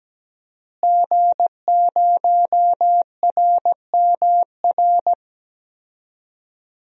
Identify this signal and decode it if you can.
Morse code, heard as a keyed tone, about 17 wpm, 700 Hz. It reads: G0RMR
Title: DXCC-100 Call Signs - 5 Characters - Repeated 3x - 17wpm